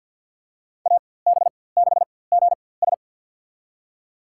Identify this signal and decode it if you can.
Morse code, heard as a keyed tone, about 50 wpm, spaced as for 23 wpm, 700 Hz. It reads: AB6GS
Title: USA Call Signs - 5 Characters - 4x Character Spacing - 50wpm